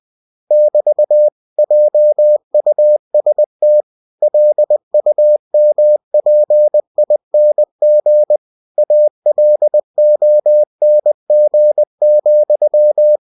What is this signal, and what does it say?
Morse code, heard as a keyed tone, about 20 wpm, 600 Hz. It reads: <BT> JUST LUMPING ALONG,
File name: cw-books / winnie-the-pooh / winnie-20wpm-page-0033